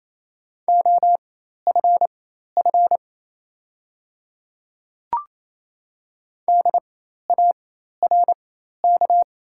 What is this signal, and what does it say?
Morse code, heard as a keyed tone, about 28 wpm, spaced as for 13 wpm, 690 Hz. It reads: OFF DARK
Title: 4x Character Spacing - Top 500 Words - 28wpm